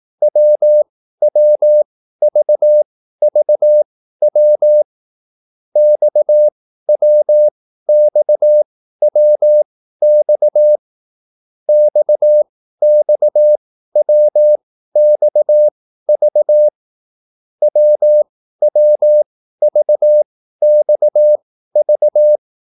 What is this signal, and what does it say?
Morse code, heard as a keyed tone, about 18 wpm, spaced as for 13 wpm, 600 Hz. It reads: WWVVW XWXWX XXWXV WWVXV
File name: QSL / CW lesson